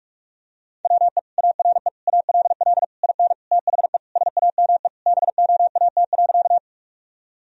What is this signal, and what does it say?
Morse code, heard as a keyed tone, about 41 wpm, 700 Hz. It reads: WE ARE ALL IN THE SAME BOAT.
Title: Sentences from Top 1000 Words - Review Part 3 - 45wpm